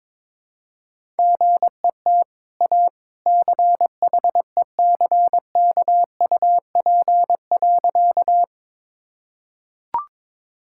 Morse code, heard as a keyed tone, about 22 wpm, 690 Hz. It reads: GET A CHECKUP.
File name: practice / 22wpm / Easy Sentences with Hints - 22wpm